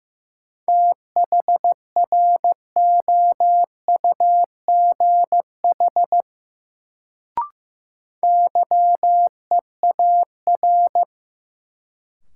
Morse code, heard as a keyed tone, about 16 wpm, 700 Hz. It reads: THROUGH YEAR
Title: Sets of 2 Words - New Word Drill - 15wpm